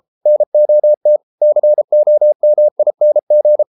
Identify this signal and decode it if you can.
Morse code, heard as a keyed tone, about 33 wpm, 600 Hz. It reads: NOT COMING